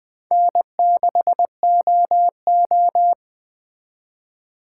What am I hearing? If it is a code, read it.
Morse code, heard as a keyed tone, about 20 wpm, 700 Hz. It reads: N6OO